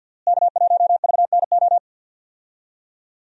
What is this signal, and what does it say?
Morse code, heard as a keyed tone, about 47 wpm, 700 Hz. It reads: K1VNO
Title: USA Call Signs - 5 Characters - Repeated 2x - 50wpm